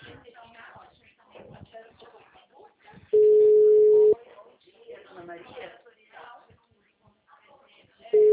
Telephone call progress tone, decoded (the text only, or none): ringback tone